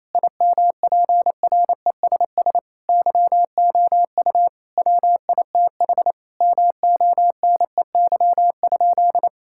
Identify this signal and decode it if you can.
Morse code, heard as a keyed tone, about 28 wpm, 700 Hz. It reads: IMPRESS YOU WITH MONEY?
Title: Sentences from Top 1600 Words 28wpm